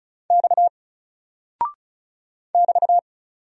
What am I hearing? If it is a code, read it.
Morse code, heard as a keyed tone, about 34 wpm, 700 Hz. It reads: X <BT>
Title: Single Character (Letter:Number:Punctuation:Prosign) Alt 35wpm